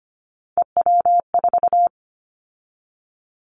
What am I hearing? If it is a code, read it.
Morse code, heard as a keyed tone, about 25 wpm, 700 Hz. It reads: EW4
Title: Callsign Prefixes for Top 100 DXCC - 25wpm